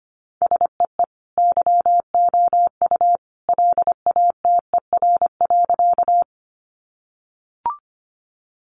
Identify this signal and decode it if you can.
Morse code, heard as a keyed tone, about 25 wpm, 700 Hz. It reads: SEE YOU LATER.
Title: Sentences from Top 700 Words 25wpm